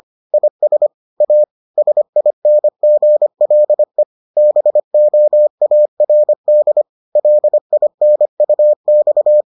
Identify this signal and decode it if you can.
Morse code, heard as a keyed tone, about 25 wpm, 600 Hz. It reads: IS A SINGLE BOARD LINUX